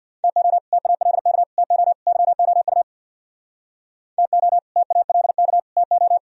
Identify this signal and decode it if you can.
Morse code, heard as a keyed tone, about 64 wpm, 700 Hz. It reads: TO TALK TO YOU TO TALK TO